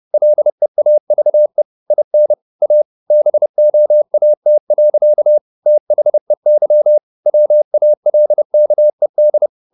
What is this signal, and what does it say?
Morse code, heard as a keyed tone, about 30 wpm, 600 Hz. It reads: LEAVE IN A BOAT. THEY WALKED
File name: the old man and the sea / the old man and the sea 1 30wpm